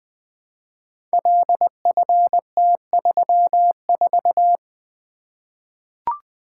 Morse code, heard as a keyed tone, about 20 wpm, 700 Hz. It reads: LFT34
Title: Code Groups - 5 Letter:Numbers - 20wpm